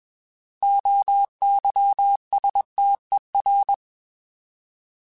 Morse code, heard as a keyed tone, about 21 wpm, 800 Hz. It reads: OYSTER
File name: Alphabetical / Alphabetical mp3s / Os @20